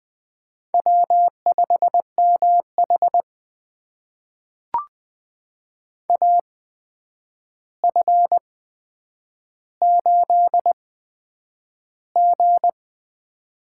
Morse code, heard as a keyed tone, about 20 wpm, spaced as for 9 wpm, 700 Hz. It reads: W5MH AF8G
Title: USA Call Signs - 4 Characters - 8x Character Spacing - 20wpm